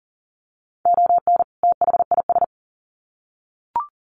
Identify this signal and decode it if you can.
Morse code, heard as a keyed tone, about 40 wpm, 700 Hz. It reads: ON THIS
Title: Sets of 2 Words 40wpm